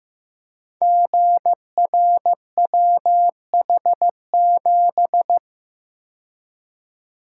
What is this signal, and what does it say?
Morse code, heard as a keyed tone, about 15 wpm, 700 Hz. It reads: GRWH7
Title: Code Groups - 5 Letter:Numbers - 15wpm